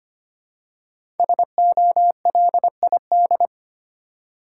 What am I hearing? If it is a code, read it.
Morse code, heard as a keyed tone, about 25 wpm, 700 Hz. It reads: SOLID
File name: 5 Letter Words - First 3 Letters Given - 25wpm